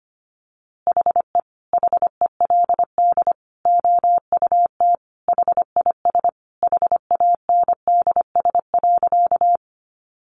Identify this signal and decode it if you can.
Morse code, heard as a keyed tone, about 25 wpm, 700 Hz. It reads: HE HELD OUT HIS HANDS.
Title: A Sound Of Thunder 25wpm